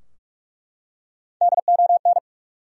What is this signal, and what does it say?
Morse code, heard as a keyed tone, about 43 wpm, 700 Hz. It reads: DON
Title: USA Common Names - Max 4 Letters - 45wpm